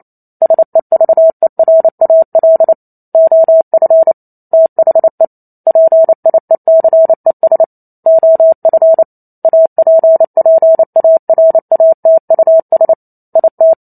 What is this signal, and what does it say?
Morse code, heard as a keyed tone, about 28 wpm, 650 Hz. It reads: SEVERAL OF THE PIECES OF APPARATUS IT